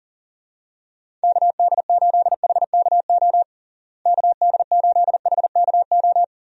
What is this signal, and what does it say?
Morse code, heard as a keyed tone, about 40 wpm, 700 Hz. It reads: KD8HKO KD8HKO
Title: USA Call Signs - 6 Characters - Repeated 3x - 40wpm